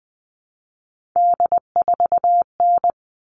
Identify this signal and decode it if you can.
Morse code, heard as a keyed tone, about 20 wpm, 700 Hz. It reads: D4N
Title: All Cut Numbers - 3-Digit 20wpm